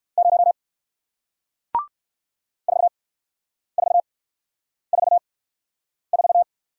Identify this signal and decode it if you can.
Morse code, heard as a keyed tone, about 56 wpm, 700 Hz. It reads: 0 4 4 4 4
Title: Speed-Racing Single Letter-Number 45wpm